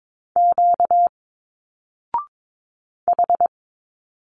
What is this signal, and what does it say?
Morse code, heard as a keyed tone, about 22 wpm, 700 Hz. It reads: Q H
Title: Single Letters 22wpm